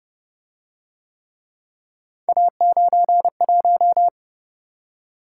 Morse code, heard as a keyed tone, about 29 wpm, 700 Hz. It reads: A91